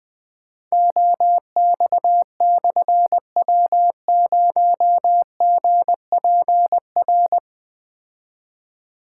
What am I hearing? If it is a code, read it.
Morse code, heard as a keyed tone, about 20 wpm, 700 Hz. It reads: OX/W0GPR
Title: Difficult Call Signs 20wpm